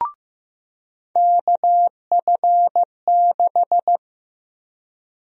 Morse code, heard as a keyed tone, about 15 wpm, 700 Hz. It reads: KF6
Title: USA Call Sign Prefixes - 15wpm